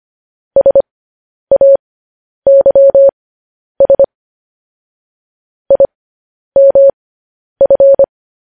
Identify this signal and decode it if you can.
Morse code, heard as a keyed tone, about 25 wpm, spaced as for 10 wpm, 550 Hz. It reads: SAYS IMF
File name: podcast / News headlines-1x-25x10